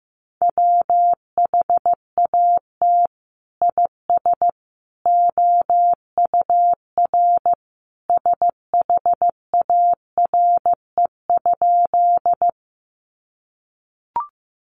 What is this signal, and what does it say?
Morse code, heard as a keyed tone, about 15 wpm, 700 Hz. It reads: WHAT IS OUR SHARE?
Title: Sentences from Top 900 Words 15wpm